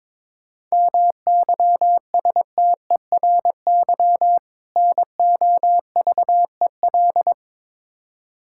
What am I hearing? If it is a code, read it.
Morse code, heard as a keyed tone, about 22 wpm, 700 Hz. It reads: MYSTERY NOVEL
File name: Difficult Words with Hints - 22wpm